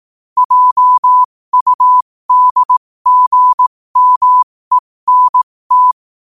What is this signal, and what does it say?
Morse code, heard as a keyed tone, about 18 wpm, spaced as for 15 wpm, 1000 Hz. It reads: JUDGMENT
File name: qotdfiles / 15wpm